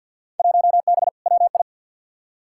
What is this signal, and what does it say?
Morse code, heard as a keyed tone, about 47 wpm, 700 Hz. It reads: 1B WI
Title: Field Day Exchanges Encore - 50wpm